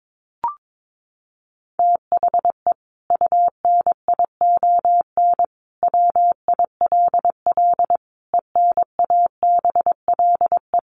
Morse code, heard as a keyed tone, about 22 wpm, 700 Hz. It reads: THE UNION WILL ENABLE
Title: Sentences from Top 1500 Words 22wpm